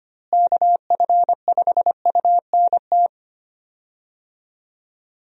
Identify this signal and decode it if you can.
Morse code, heard as a keyed tone, about 25 wpm, 700 Hz. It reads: KF5UNT
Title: USA Call Signs - 6 Characters - Repeated 3x - 25wpm